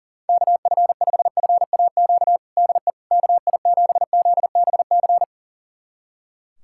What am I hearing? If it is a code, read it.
Morse code, heard as a keyed tone, about 40 wpm, 700 Hz. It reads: KF5FAQ DE KI7ZBC
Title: USA Callsign DE Callsign 40wpm